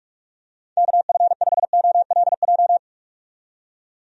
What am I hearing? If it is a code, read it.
Morse code, heard as a keyed tone, about 46 wpm, 700 Hz. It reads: KF5OLJ
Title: USA Call Signs - 6 Characters - 45wpm